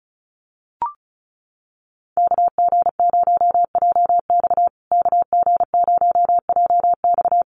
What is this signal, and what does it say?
Morse code, heard as a keyed tone, about 34 wpm, 700 Hz. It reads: KG0JX KG0JX
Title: USA Call Signs Repeated 3x 35wpm